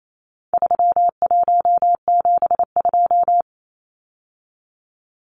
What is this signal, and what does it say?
Morse code, heard as a keyed tone, about 28 wpm, 700 Hz. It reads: 3172
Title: Numbers - 4-Digit 28wpm